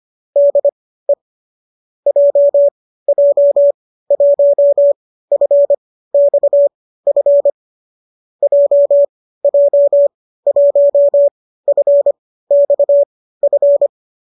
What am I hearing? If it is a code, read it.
Morse code, heard as a keyed tone, about 25 wpm, spaced as for 15 wpm, 570 Hz. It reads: DE JJ1FXF JJ1FXF
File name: qso208 15